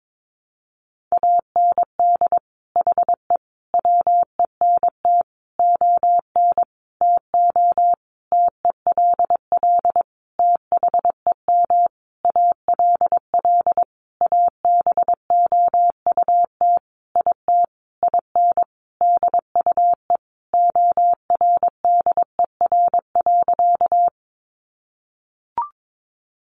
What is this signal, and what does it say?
Morse code, heard as a keyed tone, about 22 wpm, 700 Hz. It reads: AND HE WENT ON TO TELL THEM ALL ABOUT IT IN DUE ORDER.
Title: Sentences from Top 900 Words 22wpm